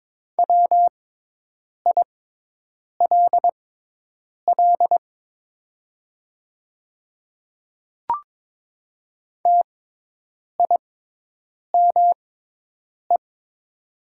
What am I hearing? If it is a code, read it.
Morse code, heard as a keyed tone, about 22 wpm, spaced as for 8 wpm, 690 Hz. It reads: WILL TIME
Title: 6x Character Spacing - Top 100 Words - 22wpm